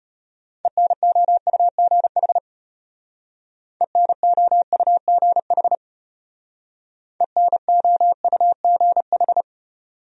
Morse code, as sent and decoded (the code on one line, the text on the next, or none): . -. --- ..- --. .... / . -. --- ..- --. .... / . -. --- ..- --. ....
ENOUGH ENOUGH ENOUGH